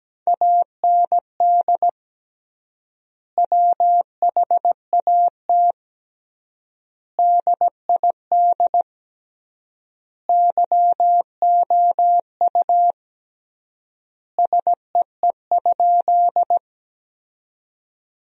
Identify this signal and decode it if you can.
Morse code, heard as a keyed tone, about 17 wpm, 700 Hz. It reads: AND WHAT DID YOU SEE?
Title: 3x Word-Spacing - Sentences from Top 100 Words - 17wpm